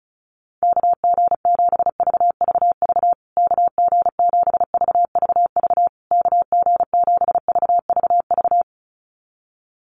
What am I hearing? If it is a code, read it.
Morse code, heard as a keyed tone, about 34 wpm, 700 Hz. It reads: KG7VVV KG7VVV KG7VVV
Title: USA Call Signs Repeated 3x 35wpm